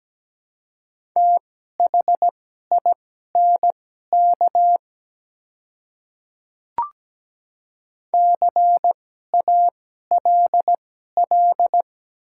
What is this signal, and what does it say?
Morse code, heard as a keyed tone, about 17 wpm, spaced as for 12 wpm, 700 Hz. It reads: THINK CALL
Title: 2x Character Spacing - Top 200 Words - 17wpm